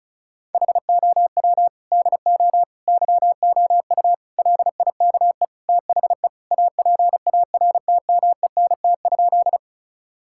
Morse code, heard as a keyed tone, about 34 wpm, 700 Hz. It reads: HOW DO YOU LIKE THE APARTMENT?
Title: Sentences from Top 1000 Words - Review Part 6 - 35wpm